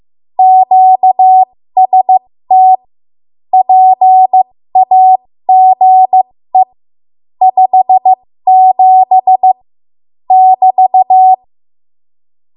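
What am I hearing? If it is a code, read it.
Morse code, heard as a keyed tone, about 15 wpm, spaced as for 13 wpm, 750 Hz. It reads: QST PAGE 57 <BT>
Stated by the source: 13 wpm Morse code practice archive transmission.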